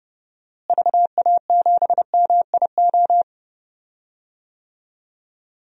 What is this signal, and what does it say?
Morse code, heard as a keyed tone, about 30 wpm, 700 Hz. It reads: VA7MIO
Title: DXCC-100 Call Signs - 6 Characters - Repeated 3x - 30wpm